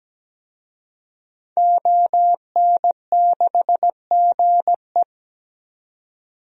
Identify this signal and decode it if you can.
Morse code, heard as a keyed tone, about 17 wpm, 700 Hz. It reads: ON6GE
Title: DXCC-100 Call Signs - 5 Characters - 17wpm